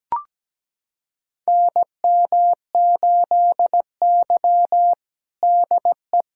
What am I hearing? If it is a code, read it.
Morse code, heard as a keyed tone, about 17 wpm, 700 Hz. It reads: NM8Y DE
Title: USA Callsign DE Callsign - 4 Characters - 17wpm